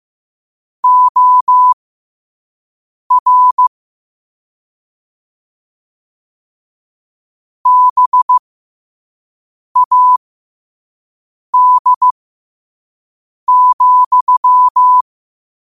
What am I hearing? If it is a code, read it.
Morse code, heard as a keyed tone, about 15 wpm, spaced as for 5 wpm, 1000 Hz. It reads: OR BAD,
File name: qotdfiles / 5wpm